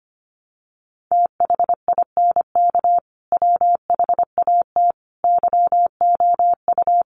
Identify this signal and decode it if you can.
Morse code, heard as a keyed tone, about 25 wpm, 700 Hz. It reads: THINK WHAT YOU
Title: Sentences from Top 200 Words 25wpm